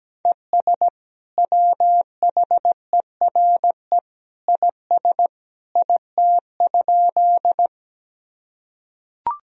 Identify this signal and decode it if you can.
Morse code, heard as a keyed tone, about 17 wpm, 700 Hz. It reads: ES WHERE IS IT?
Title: Sentences from Top 200 Words and Abbreviations 17wpm